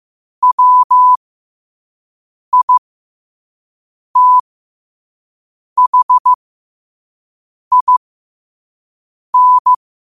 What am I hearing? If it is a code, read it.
Morse code, heard as a keyed tone, about 15 wpm, spaced as for 5 wpm, 1000 Hz. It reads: WITHIN
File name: qotdfiles / 5wpm